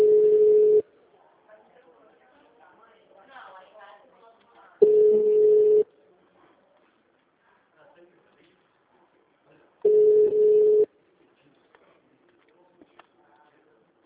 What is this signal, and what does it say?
Ringback tone (Europe, 425 Hz), with speech.